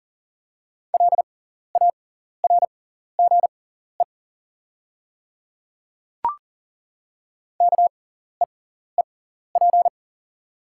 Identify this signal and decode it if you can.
Morse code, heard as a keyed tone, about 37 wpm, spaced as for 14 wpm, 700 Hz. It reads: LARGE KEEP
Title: 6x Character Spacing - Top 300 Words - 40wpm